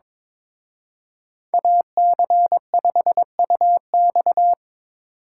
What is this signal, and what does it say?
Morse code, heard as a keyed tone, about 24 wpm, 700 Hz. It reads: AC5UX
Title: Speed Racing - USA 2x2 Call Signs 22wpm